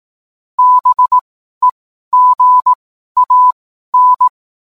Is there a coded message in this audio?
Yes: Morse code from 0.6 to 4.3 s, heard as a keyed tone, about 18 wpm, spaced as for 12 wpm, 1000 Hz. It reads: BEGAN